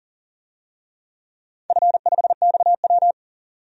Morse code, heard as a keyed tone, about 40 wpm, 700 Hz. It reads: F5XW
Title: DXCC-100 Call Signs - 4 Characters - 40wpm